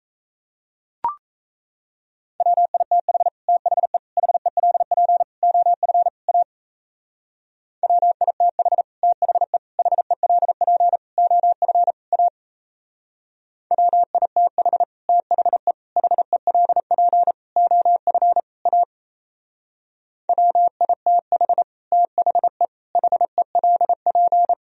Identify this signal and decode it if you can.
Morse code, heard as a keyed tone, about 35 wpm, 700 Hz. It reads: WITH THE HELP OF A WITH THE HELP OF A WITH THE HELP OF A WITH THE HELP
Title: Speed-Racing - Sets of 5 Words 28wpm